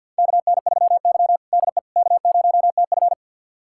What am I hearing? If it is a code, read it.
Morse code, heard as a keyed tone, about 52 wpm, 700 Hz. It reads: KN3Y DE K0TF
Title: USA Callsign DE Callsign - 4 Characters - 50wpm